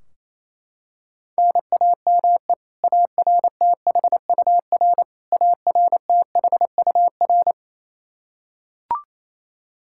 Morse code, heard as a keyed tone, about 28 wpm, 700 Hz. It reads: NAME ARTHUR ARTHUR
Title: Practice QSO - Exchange 1 - Beginning - 28wpm